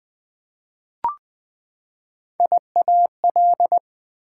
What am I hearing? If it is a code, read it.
Morse code, heard as a keyed tone, about 20 wpm, 700 Hz. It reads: IAL
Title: Prefixes and Suffixes 20wpm